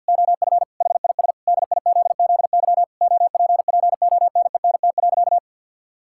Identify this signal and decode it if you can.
Morse code, heard as a keyed tone, about 49 wpm, 700 Hz. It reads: OF HIS DIZZY OPPONENT.
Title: Pangram Challenge v2 - 50wpm